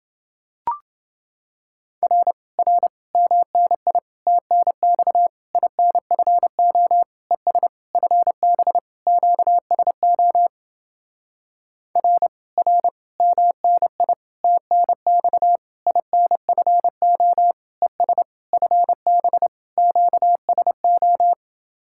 Morse code, heard as a keyed tone, about 30 wpm, 700 Hz. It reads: R R MNI TNX INFO ES FB QSO R R MNI TNX INFO ES FB QSO